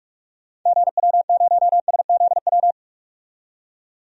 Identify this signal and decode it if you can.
Morse code, heard as a keyed tone, about 43 wpm, 700 Hz. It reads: GW0SZW
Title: DXCC-100 Call Signs - 6 Characters - 45wpm